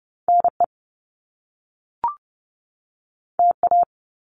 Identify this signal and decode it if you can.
Morse code, heard as a keyed tone, about 30 wpm, 700 Hz. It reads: NE TA
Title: Top Two Letter Combinations in Words 30wpm